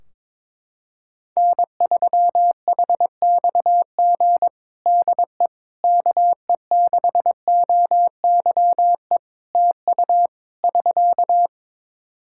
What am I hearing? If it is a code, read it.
Morse code, heard as a keyed tone, about 22 wpm, 700 Hz. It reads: N3HXG DE KE6OYE TU <SK>
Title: Practice QSO - Exchange 4 - Ending - 22wpm